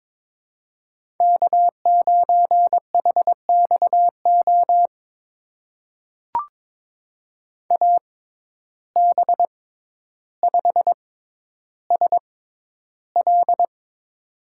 Morse code, heard as a keyed tone, about 23 wpm, spaced as for 11 wpm, 700 Hz. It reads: K9HXO AB5SL
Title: USA Call Signs - 5 Characters - 6x Character Spacing - 22wpm